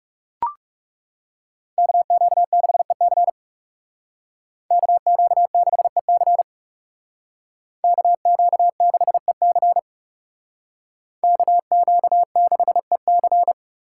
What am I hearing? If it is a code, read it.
Morse code, heard as a keyed tone, about 35 wpm, 700 Hz. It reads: KQ6EC KQ6EC KQ6EC KQ6EC
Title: Speed Racing - USA 2x2 Call Signs 30wpm